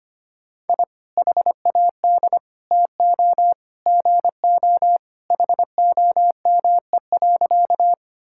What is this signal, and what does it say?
Morse code, heard as a keyed tone, about 25 wpm, 700 Hz. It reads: I HAD TO GO HOME.
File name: practice / 25wpm / Sentences with 3-Word Phrases - 25wpm